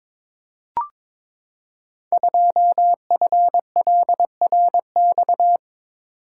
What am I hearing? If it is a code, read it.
Morse code, heard as a keyed tone, about 23 wpm, 700 Hz. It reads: 2FLRX
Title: Code Groups - 5 Letter:Numbers - 22wpm